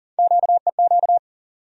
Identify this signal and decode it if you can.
Morse code, heard as a keyed tone, about 40 wpm, 700 Hz. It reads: QEQ
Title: USA Call Sign Postfixes - 3 Letters 40wpm